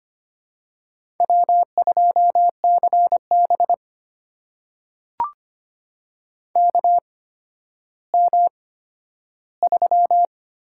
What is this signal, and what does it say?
Morse code, heard as a keyed tone, about 25 wpm, spaced as for 12 wpm, 700 Hz. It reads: W2CB KM3
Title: USA Call Signs - 8x Character Spacing - 25wpm